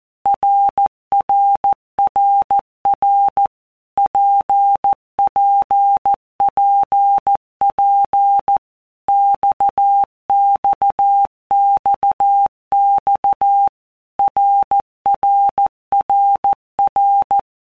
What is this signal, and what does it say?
Morse code, heard as a keyed tone, about 14 wpm, 800 Hz. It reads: RRRR PPPP XXXX RRRR
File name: All Groups 1-minute Blocks / Groups@13 / RKPX